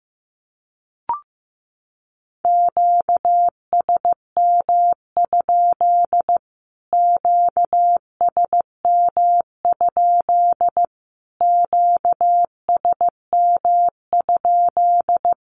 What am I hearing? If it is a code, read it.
Morse code, heard as a keyed tone, about 15 wpm, 700 Hz. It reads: QSM? QSM? QSM?
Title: Q-Codes Repeated 3x 15wpm